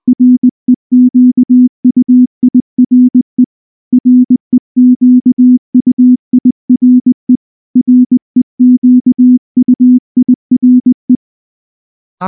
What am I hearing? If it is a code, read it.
Morse code, heard as a keyed tone, about 20 wpm, 250 Hz. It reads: REQUIRE REQUIRE REQUIRE